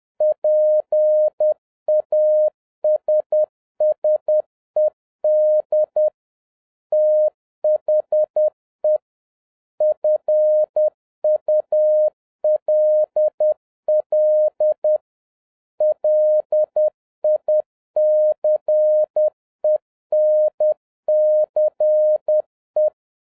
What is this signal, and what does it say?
Morse code, heard as a keyed tone, about 10 wpm, 610 Hz. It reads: PASSED THE FULL LICENCE